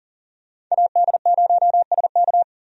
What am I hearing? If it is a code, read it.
Morse code, heard as a keyed tone, about 40 wpm, 700 Hz. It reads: AD0SK
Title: USA Call Signs - 5 Characters - 40wpm